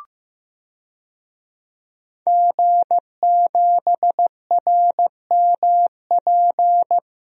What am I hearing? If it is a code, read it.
Morse code, heard as a keyed tone, about 15 wpm, 700 Hz. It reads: G7RMP
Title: DXCC-100 Call Signs - 5 Characters - Repeated 3x - 15wpm